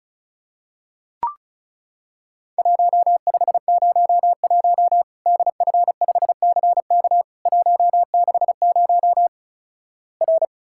Morse code, heard as a keyed tone, about 36 wpm, 700 Hz. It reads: 1501 DF5CK 160 R
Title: Worked All Europe DX Contest - QTC Exchange - 35wpm